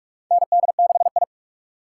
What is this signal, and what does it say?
Morse code, heard as a keyed tone, about 46 wpm, 700 Hz. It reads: ND6I
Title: USA Call Signs - 4 Characters - 4x Character Spacing - 45wpm